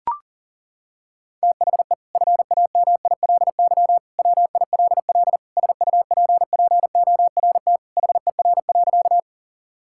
Morse code, heard as a keyed tone, about 37 wpm, 700 Hz. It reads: THE FAMILY WILL SUPPORT HER.
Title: Sentences from Top 1000 Words - Review Part 5 - 40wpm